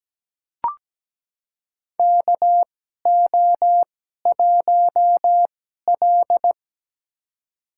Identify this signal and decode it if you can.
Morse code, heard as a keyed tone, about 17 wpm, spaced as for 12 wpm, 700 Hz. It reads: KO1L
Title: USA Call Signs - 4 Characters - 2x Character Spacing - 17wpm